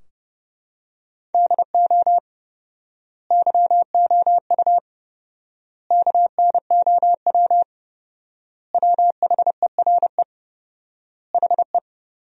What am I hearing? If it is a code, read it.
Morse code, heard as a keyed tone, about 32 wpm, 700 Hz. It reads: DO YOU KNOW WHERE HE
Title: 4x Word-Spacing - Sentences from Top 300 Words - 30wpm